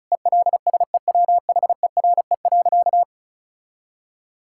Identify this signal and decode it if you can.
Morse code, heard as a keyed tone, about 36 wpm, 700 Hz. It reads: ELSEWHERE.
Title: Sentences from Top 1600 Words 35wpm